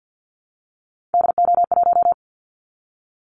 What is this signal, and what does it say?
Morse code, heard as a keyed tone, about 54 wpm, 700 Hz. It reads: DO1
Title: Callsign Prefixes for Top 25 DXCC - Repeated 3x - 50wpm